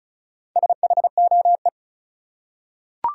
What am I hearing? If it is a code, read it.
Morse code, heard as a keyed tone, about 34 wpm, 700 Hz. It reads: SHOE
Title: Four Letter Words - 35wpm